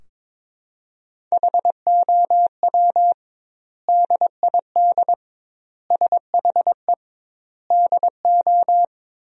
Morse code, heard as a keyed tone, about 22 wpm, 700 Hz. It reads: HOW DID SHE DO
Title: 2x Word-Spacing - Sentences from Top 100 Words - 22wpm